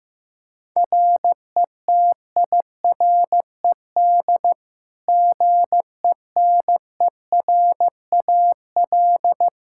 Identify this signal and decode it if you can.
Morse code, heard as a keyed tone, about 16 wpm, 700 Hz. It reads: RETIRED GENERAL
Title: Practice QSO - Exchange 3 - More Info - 15wpm